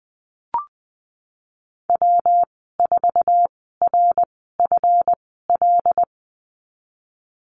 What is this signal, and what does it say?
Morse code, heard as a keyed tone, about 20 wpm, spaced as for 14 wpm, 700 Hz. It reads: W4RFL
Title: 154 Lesson 25g - Letter p - Call Signs 20wpm